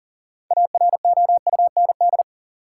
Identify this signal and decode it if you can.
Morse code, heard as a keyed tone, about 37 wpm, 700 Hz. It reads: AROUND
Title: Sentences from Top 600 Words - New Word Drill - 40wpm